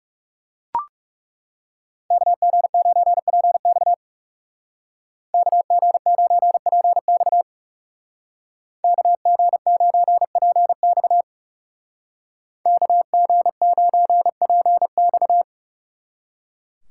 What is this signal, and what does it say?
Morse code, heard as a keyed tone, about 35 wpm, 700 Hz. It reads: KG9PX KG9PX KG9PX KG9PX
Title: Speed Racing - USA 2x2 Call Signs 30wpm